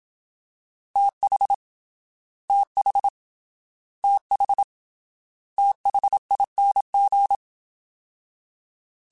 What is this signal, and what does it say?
Morse code, heard as a keyed tone, about 27 wpm, 800 Hz. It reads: TH TH TH THING
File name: TH / TH mp3s / TH Prefixes 3x2 Twice @25